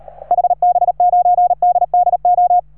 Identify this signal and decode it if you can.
Morse code, heard as a keyed tone, about 36 wpm, 710 Hz. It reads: HB9DDO
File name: RS3A-20